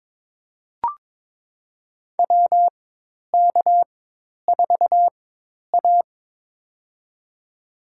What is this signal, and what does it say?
Morse code, heard as a keyed tone, about 22 wpm, spaced as for 10 wpm, 700 Hz. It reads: WK4A
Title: USA Call Signs - 4x Character Spacing - 22wpm